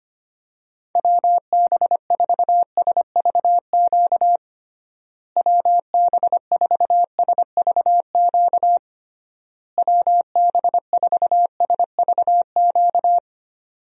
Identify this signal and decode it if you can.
Morse code, heard as a keyed tone, about 25 wpm, 700 Hz. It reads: WB4SVQ WB4SVQ WB4SVQ
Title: USA Call Signs - 6 Characters - Repeated 3x - 25wpm